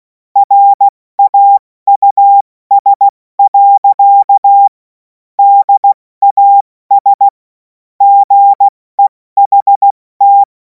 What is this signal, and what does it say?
Morse code, heard as a keyed tone, about 16 wpm, 800 Hz. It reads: RAUS. DAS GEHT
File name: Morsekurs / Witze / Witz 17. März 2026 70 F80